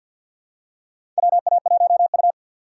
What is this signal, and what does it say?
Morse code, heard as a keyed tone, about 53 wpm, 700 Hz. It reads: WA1U
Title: USA Call Signs - 4 Characters - 2x Character Spacing - 50wpm